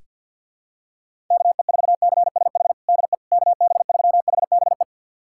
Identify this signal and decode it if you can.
Morse code, heard as a keyed tone, about 47 wpm, 700 Hz. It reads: KE4CSH DE KB3HBE
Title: USA Callsign DE Callsign 50wpm